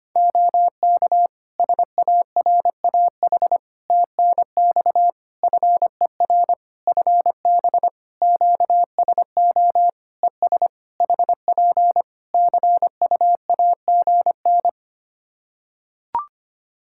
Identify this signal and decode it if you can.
Morse code, heard as a keyed tone, about 25 wpm, 700 Hz. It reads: OK SARAH TNX FER FB QSO ES HP CUAGN
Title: Practice QSO - Exchange 4 - Ending - 25wpm